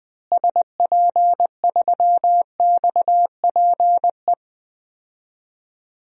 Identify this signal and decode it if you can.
Morse code, heard as a keyed tone, about 20 wpm, 700 Hz. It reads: SP3XPE
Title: DXCC-100 Call Signs - 6 Characters - Repeated 2x - 20wpm